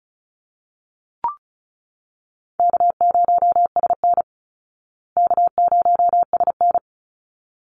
Morse code, heard as a keyed tone, about 34 wpm, 700 Hz. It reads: K0SN K0SN
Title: Call Signs Repeated 3x - 35wpm